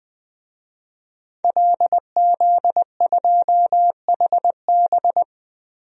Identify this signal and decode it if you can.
Morse code, heard as a keyed tone, about 20 wpm, 700 Hz. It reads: LZ2HB
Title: DXCC-100 Call Signs - 5 Characters - 20wpm